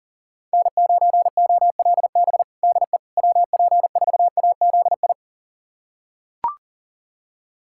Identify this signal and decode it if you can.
Morse code, heard as a keyed tone, about 37 wpm, 700 Hz. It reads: N9OLB DE WP4AZI
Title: USA Callsign DE Callsign 40wpm